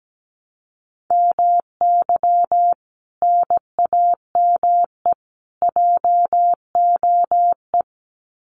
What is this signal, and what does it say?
Morse code, heard as a keyed tone, about 17 wpm, 700 Hz. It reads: MY NAME JOE
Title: Common QSO Phrases 17wpm